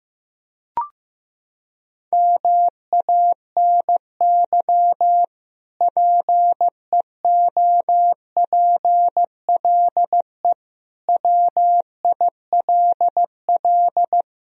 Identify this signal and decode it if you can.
Morse code, heard as a keyed tone, about 16 wpm, 700 Hz. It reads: MANY PEOPLE WILL
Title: Sentences from Top 1000 Words - Review Part 1 - 15wpm